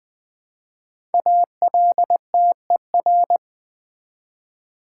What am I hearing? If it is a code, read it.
Morse code, heard as a keyed tone, about 20 wpm, 700 Hz. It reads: ALTER
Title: 5 Letter Words - First 3 Letters Given - 20wpm